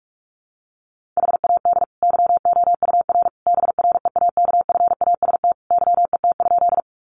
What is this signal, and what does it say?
Morse code, heard as a keyed tone, about 44 wpm, 700 Hz. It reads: HAD YOUR BREAKFAST YET?
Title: Sentences from Top 1500 Words 45wpm